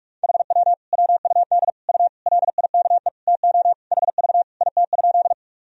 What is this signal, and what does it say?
Morse code, heard as a keyed tone, about 41 wpm, 700 Hz. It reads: HW WUD U LIKE TO HV IT?
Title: Sentences from Top 100 Words and Abbreviations 45wpm